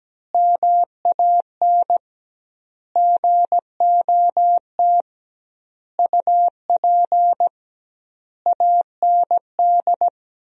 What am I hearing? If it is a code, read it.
Morse code, heard as a keyed tone, about 17 wpm, 700 Hz. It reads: MAN GOT UP AND
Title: 2x Word-Spacing - Sentences from Top 300 Words - 17wpm